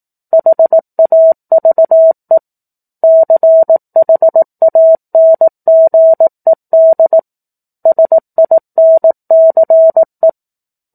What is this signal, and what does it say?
Morse code, heard as a keyed tone, about 18 wpm, 650 Hz. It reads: HAVE CHANGED SINCE
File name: 04-Chapter-04 18 WPM